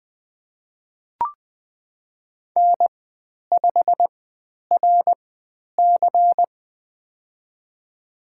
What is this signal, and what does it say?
Morse code, heard as a keyed tone, about 20 wpm, spaced as for 10 wpm, 700 Hz. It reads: N5RC
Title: 109 Lesson 18g - Letter m - Call Signs 20wpm